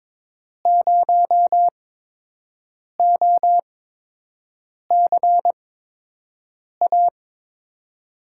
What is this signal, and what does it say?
Morse code, heard as a keyed tone, about 22 wpm, spaced as for 6 wpm, 700 Hz. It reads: 0OCA
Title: USA Call Signs - 6 Characters - 8x Character Spacing - 22wpm